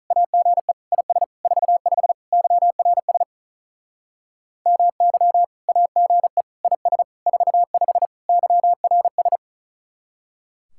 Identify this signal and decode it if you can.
Morse code, heard as a keyed tone, about 38 wpm, 700 Hz. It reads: AGE IS 45 YRS MY AGE IS 45 YRS
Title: Speed-Racing Common QSO Phrases Encore - 35wpm